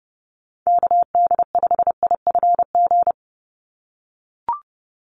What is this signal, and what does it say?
Morse code, heard as a keyed tone, about 28 wpm, 700 Hz. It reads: KD5IFG
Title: USA Call Signs 30wpm